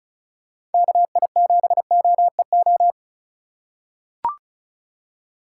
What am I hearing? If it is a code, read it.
Morse code, heard as a keyed tone, about 36 wpm, 700 Hz. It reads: KI7OEO
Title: USA Call Signs - 6 Characters - 2x Character Spacing - 35wpm